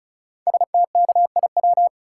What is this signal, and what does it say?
Morse code, heard as a keyed tone, about 36 wpm, 690 Hz. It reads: STKIW
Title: Code Groups - 5 Letters - 35wpm